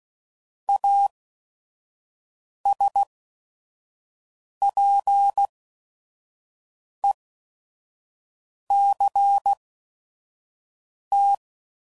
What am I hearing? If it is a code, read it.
Morse code, heard as a keyed tone, about 16 wpm, spaced as for 5 wpm, 800 Hz. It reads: ASPECT
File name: CW Code Groups @5WPM / Technical Subjects / SwPwrSup